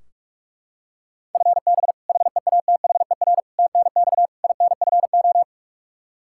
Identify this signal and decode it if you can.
Morse code, heard as a keyed tone, about 45 wpm, 700 Hz. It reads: FB HEATHER TNX INFO
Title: Practice QSO - Exchange 3 - More Info - 45wpm